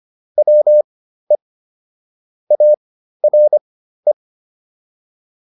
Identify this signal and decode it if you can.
Morse code, heard as a keyed tone, about 25 wpm, spaced as for 13 wpm, 600 Hz. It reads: WE ARE